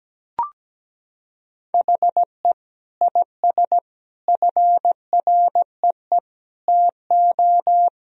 Morse code, heard as a keyed tone, about 17 wpm, 700 Hz. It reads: HE IS FREE TO
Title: Sentences from Top 1000 Words - Review Part 4 - 17wpm